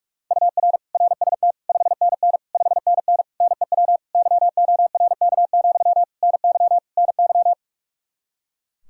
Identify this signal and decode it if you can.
Morse code, heard as a keyed tone, about 46 wpm, 700 Hz. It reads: UR RST 5NN 5NN NEW YORK, NY NY
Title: Practice QSO - Exchange 1 - Beginning - 45wpm